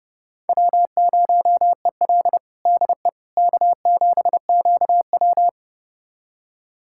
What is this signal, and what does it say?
Morse code, heard as a keyed tone, about 32 wpm, 700 Hz. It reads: W0EL DE K7QW
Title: USA Callsign DE Callsign - 4 Characters - 30wpm